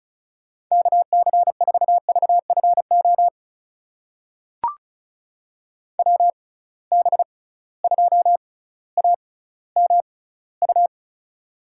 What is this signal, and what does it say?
Morse code, heard as a keyed tone, about 36 wpm, spaced as for 18 wpm, 700 Hz. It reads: KC4VFO WB2AMU
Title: USA Call Signs - 6 Characters - 6x Character Spacing - 35wpm